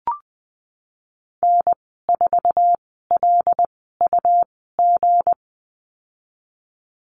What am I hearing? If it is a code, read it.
Morse code, heard as a keyed tone, about 19 wpm, spaced as for 14 wpm, 700 Hz. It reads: N4LUG
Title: 173 Lesson 28g - Number 9 - Call Signs 20wpm